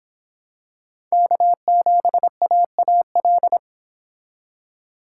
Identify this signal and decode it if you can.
Morse code, heard as a keyed tone, about 23 wpm, 700 Hz. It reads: K7AAL